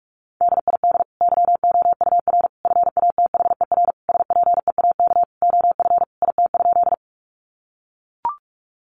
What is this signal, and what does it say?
Morse code, heard as a keyed tone, about 44 wpm, 700 Hz. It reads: DID YOUR FATHER SPEAK OF IT?
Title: Sentences from Top 1100 Words 45wpm